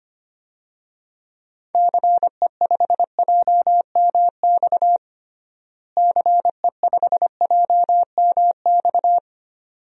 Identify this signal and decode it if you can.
Morse code, heard as a keyed tone, about 25 wpm, 700 Hz. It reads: CE5JMX CE5JMX
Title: DXCC-100 Call Signs - 6 Characters - Repeated 2x - 25wpm